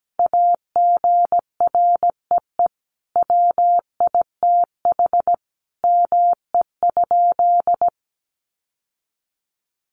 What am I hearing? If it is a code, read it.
Morse code, heard as a keyed tone, about 17 wpm, 700 Hz. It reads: AGREE WITH ME?
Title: Sentences from Top 1100 Words 17wpm